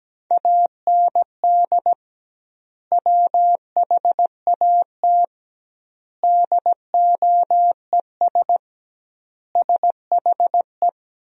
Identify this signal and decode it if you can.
Morse code, heard as a keyed tone, about 17 wpm, 700 Hz. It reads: AND WHAT DOES SHE
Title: 2x Word-Spacing - Sentences from Top 400 Words - 17wpm